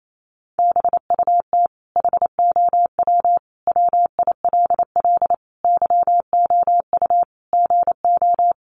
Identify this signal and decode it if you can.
Morse code, heard as a keyed tone, about 28 wpm, 700 Hz. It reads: BUT HOW WILL YOU GO
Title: Sentences from Top 1100 Words 28wpm